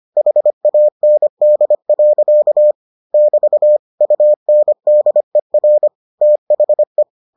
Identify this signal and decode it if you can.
Morse code, heard as a keyed tone, about 25 wpm, 600 Hz. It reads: HAND. <BT> UNDER THE